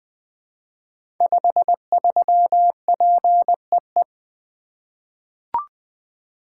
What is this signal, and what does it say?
Morse code, heard as a keyed tone, about 20 wpm, 700 Hz. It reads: 53PEE
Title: Code Groups - 5 Letter:Numbers - 20wpm